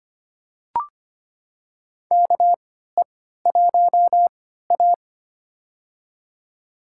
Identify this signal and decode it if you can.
Morse code, heard as a keyed tone, about 25 wpm, spaced as for 14 wpm, 700 Hz. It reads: KE1A
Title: USA Call Signs - 4 Characters - 3x Character Spacing - 25wpm